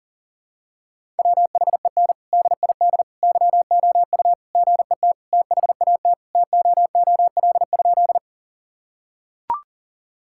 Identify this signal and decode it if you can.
Morse code, heard as a keyed tone, about 40 wpm, 700 Hz. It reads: WHEN DID YOU GET THAT TOOL?
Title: Sentences from Top 1000 Words - Review Part 6 - 40wpm